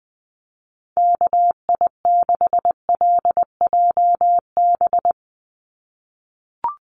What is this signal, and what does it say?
Morse code, heard as a keyed tone, about 20 wpm, 700 Hz. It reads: KI6LJB
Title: USA Call Signs 20wpm